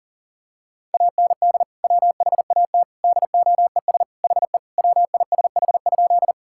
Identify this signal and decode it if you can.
Morse code, heard as a keyed tone, about 40 wpm, 700 Hz. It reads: AND WHAT DOES HE WISH?
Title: Sentences from Top 1600 Words 40wpm